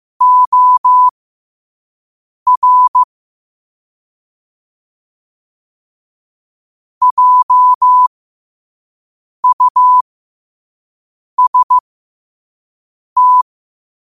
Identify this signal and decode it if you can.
Morse code, heard as a keyed tone, about 15 wpm, spaced as for 5 wpm, 1000 Hz. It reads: OR JUST
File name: qotdfiles / 5wpm